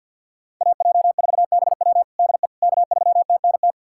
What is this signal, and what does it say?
Morse code, heard as a keyed tone, about 50 wpm, 700 Hz. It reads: AJ4BW DE K3TNT
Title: USA Callsign DE Callsign - 5 Characters - 50wpm